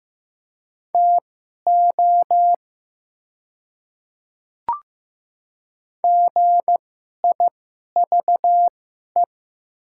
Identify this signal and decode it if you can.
Morse code, heard as a keyed tone, about 15 wpm, spaced as for 11 wpm, 700 Hz. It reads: TO GIVE
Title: 2x Character Spacing - Top 200 Words - 15wpm